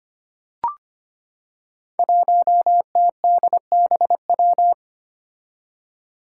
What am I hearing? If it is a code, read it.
Morse code, heard as a keyed tone, about 25 wpm, 700 Hz. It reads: 1TDBW
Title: Five Challenging Letter-Number Combinations 25wpm